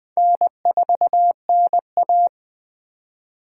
Morse code, heard as a keyed tone, about 20 wpm, 700 Hz. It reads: N4NA
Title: USA Call Signs - 3x Character Spacing - 20wpm